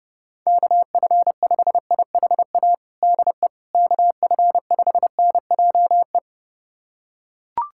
Morse code, heard as a keyed tone, about 30 wpm, 700 Hz. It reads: KF5IHA DE KF5NJE
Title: USA Callsign DE Callsign 30wpm